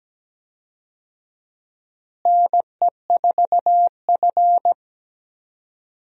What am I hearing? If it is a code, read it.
Morse code, heard as a keyed tone, about 17 wpm, 700 Hz. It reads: NE4F
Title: USA Call Signs - 4 Characters - Repeated 2x - 17wpm